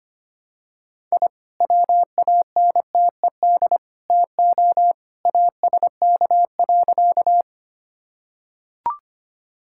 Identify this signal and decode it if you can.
Morse code, heard as a keyed tone, about 25 wpm, 700 Hz. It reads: I WANTED TO ASK.
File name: practice / 25wpm / Sentences with 3-Word Phrases - 25wpm